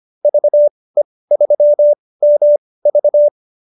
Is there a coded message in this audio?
Yes: Morse code from 0.2 to 3.3 s, heard as a keyed tone, about 25 wpm, spaced as for 18 wpm, 590 Hz. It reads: VE3MV